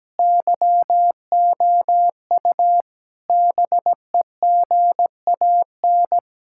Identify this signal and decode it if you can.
Morse code, heard as a keyed tone, about 17 wpm, 700 Hz. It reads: YOU BEGAN
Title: Sentences from Top 1600 Words 17wpm